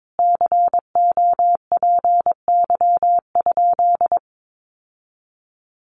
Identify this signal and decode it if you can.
Morse code, heard as a keyed tone, about 22 wpm, 700 Hz. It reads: COPY?
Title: Sentences from Top 1400 Words 22wpm